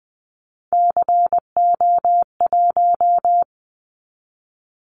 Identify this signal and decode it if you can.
Morse code, heard as a keyed tone, about 20 wpm, 700 Hz. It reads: CO1
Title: Callsign Prefixes for Top 100 DXCC - 20wpm